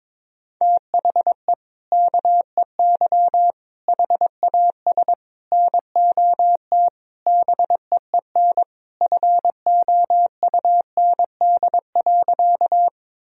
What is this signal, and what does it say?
Morse code, heard as a keyed tone, about 22 wpm, 700 Hz. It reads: THE KEY HAS NOT BEEN FOUND.
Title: Sentences from Top 1000 Words - Review Part 4 - 22wpm